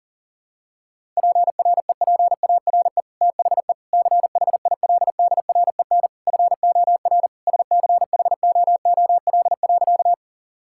Morse code, heard as a keyed tone, about 37 wpm, 700 Hz. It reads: PREPARE THE CHILDREN FOR SCHOOL.
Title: Sentences from Top 1000 Words - Review Part 6 - 40wpm